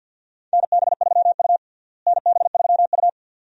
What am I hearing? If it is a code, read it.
Morse code, heard as a keyed tone, about 51 wpm, 700 Hz. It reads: NB3U NB3U
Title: USA Call Signs - 4 Characters - Repeated 2x - 50wpm